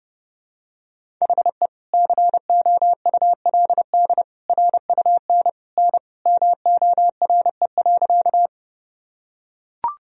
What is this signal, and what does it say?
Morse code, heard as a keyed tone, about 30 wpm, 700 Hz. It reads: HE COULD RUN N MORE.
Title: Sentences from Top 300 Words and Abbreviations 30wpm